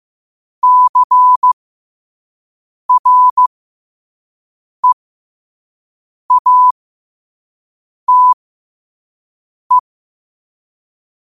Morse code, heard as a keyed tone, about 15 wpm, spaced as for 5 wpm, 1000 Hz. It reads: CREATE
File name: qotdfiles / 5wpm